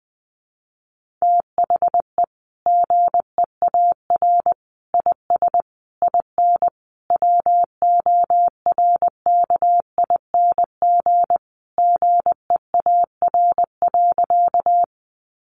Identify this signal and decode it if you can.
Morse code, heard as a keyed tone, about 20 wpm, 700 Hz. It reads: THE GEAR IS IN WORKING GEAR.
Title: Sentences from Top 1400 Words 20wpm